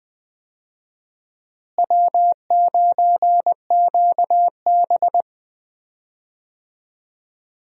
Morse code, heard as a keyed tone, about 21 wpm, 700 Hz. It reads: W9QB
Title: USA Call Signs - 4 Characters - Repeated 2x - 20wpm